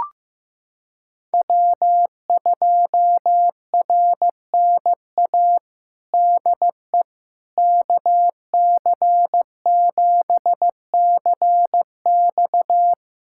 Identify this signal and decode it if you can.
Morse code, heard as a keyed tone, about 15 wpm, 700 Hz. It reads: W2RNA DE KC7CX
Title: USA Callsign DE Callsign - 5 Characters - 15wpm